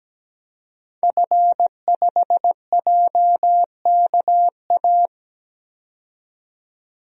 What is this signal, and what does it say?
Morse code, heard as a keyed tone, about 17 wpm, 700 Hz. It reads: F5JKA
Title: DXCC-100 Call Signs - 5 Characters - 17wpm